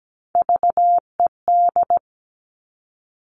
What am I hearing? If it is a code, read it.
Morse code, heard as a keyed tone, about 17 wpm, 700 Hz. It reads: VED
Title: Top Three Letter Combinations in Words 17wpm